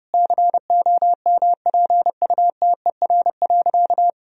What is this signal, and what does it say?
Morse code, heard as a keyed tone, about 30 wpm, 700 Hz. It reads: COMPUTER.
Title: Sentences from Top 1000 Words - Review Part 3 - 30wpm